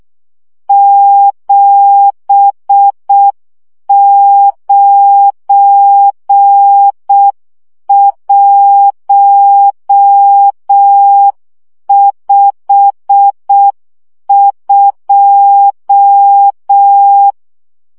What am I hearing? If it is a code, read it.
Morse code, heard as a keyed tone, about 6 wpm, 800 Hz. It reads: 79152
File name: morze vizsga 2011-3 5